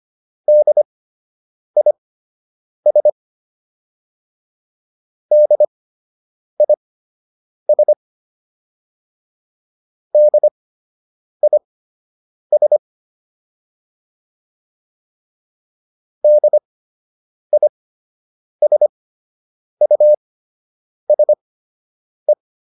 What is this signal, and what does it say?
Morse code, heard as a keyed tone, about 25 wpm, spaced as for 8 wpm, 600 Hz. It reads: DIS DIS DIS DISUSE